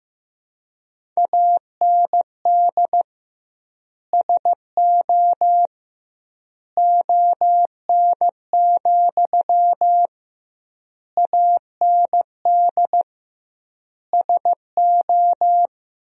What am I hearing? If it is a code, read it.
Morse code, heard as a keyed tone, about 15 wpm, 700 Hz. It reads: AND SO ON, AND SO
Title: 2x Word-Spacing - Sentences from Top 100 Words - 15wpm